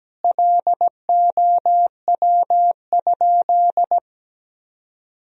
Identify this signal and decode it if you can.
Morse code, heard as a keyed tone, about 17 wpm, 700 Hz. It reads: LOW?
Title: Sentences from Top 1000 Words - Review Part 4 - 17wpm